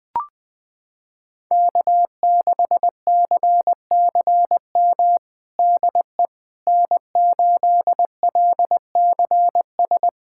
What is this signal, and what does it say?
Morse code, heard as a keyed tone, about 20 wpm, 700 Hz. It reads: K6CCM DE N8LCS
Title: USA Callsign DE Callsign - 5 Characters - 20wpm